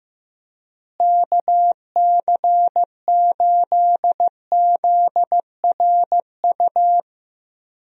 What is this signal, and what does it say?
Morse code, heard as a keyed tone, about 15 wpm, 700 Hz. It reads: KC8ZRU
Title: USA Call Signs - 6 Characters - Repeated 3x - 15wpm